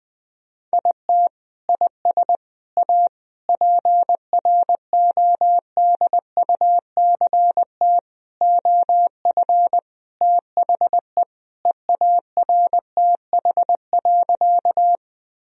Morse code, heard as a keyed tone, about 20 wpm, 700 Hz. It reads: IT IS A PRODUCT OF THE EARTH.
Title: Sentences from Top 1000 Words - Review Part 2 - 20wpm